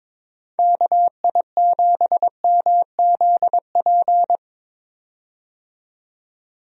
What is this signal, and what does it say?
Morse code, heard as a keyed tone, about 22 wpm, 700 Hz. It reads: KI7MZP
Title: USA Call Signs - 6 Characters - 22wpm